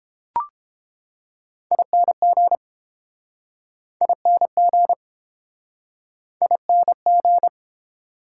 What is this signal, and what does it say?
Morse code, heard as a keyed tone, about 30 wpm, 700 Hz. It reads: ING ING ING